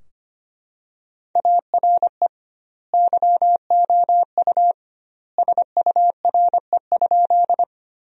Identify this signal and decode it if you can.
Morse code, heard as a keyed tone, about 25 wpm, 700 Hz. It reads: ARE YOU SURE?
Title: 2x Word-Spacing - Sentences from Top 300 Words - 25wpm